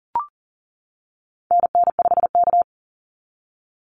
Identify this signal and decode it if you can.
Morse code, heard as a keyed tone, about 40 wpm, 700 Hz. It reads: NN5K
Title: NA SOTA Activator Callsigns 40wpm